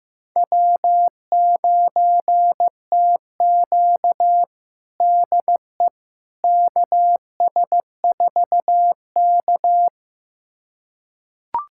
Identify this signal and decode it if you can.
Morse code, heard as a keyed tone, about 15 wpm, 700 Hz. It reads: W9TQ DE KS4K
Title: USA Callsign DE Callsign - 4 Characters - 15wpm